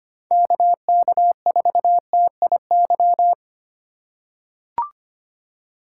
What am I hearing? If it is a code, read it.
Morse code, heard as a keyed tone, about 25 wpm, 700 Hz. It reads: KK4TIY
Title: USA Call Signs - 6 Characters - 3x Character Spacing - 25wpm